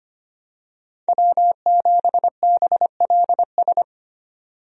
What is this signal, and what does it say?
Morse code, heard as a keyed tone, about 25 wpm, 700 Hz. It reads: W7BLS
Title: USA Call Signs - 5 Characters - 25wpm